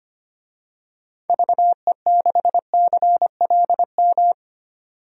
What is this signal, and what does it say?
Morse code, heard as a keyed tone, about 25 wpm, 700 Hz. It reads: VE6CLM
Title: DXCC-100 Call Signs - 6 Characters - 25wpm